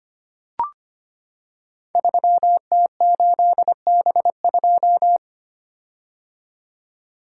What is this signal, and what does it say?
Morse code, heard as a keyed tone, about 25 wpm, 700 Hz. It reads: 3T8B2
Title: Code Groups - 5 Letter:Numbers - 25wpm